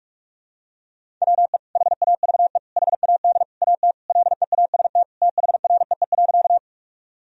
Morse code, heard as a keyed tone, about 46 wpm, 700 Hz. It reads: WE HAVE HAD AT LEAST THREE.
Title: Sentences from Top 1000 Words - Review Part 1 - 45wpm